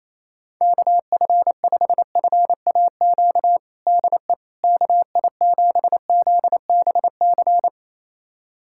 Morse code, heard as a keyed tone, about 28 wpm, 700 Hz. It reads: KF5FAQ DE KI7ZBC
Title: USA Callsign DE Callsign 28wpm